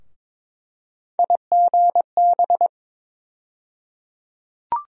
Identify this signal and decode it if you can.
Morse code, heard as a keyed tone, about 22 wpm, 700 Hz. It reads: IGB
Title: USA Call Sign Postfixes - 3 Letters 22wpm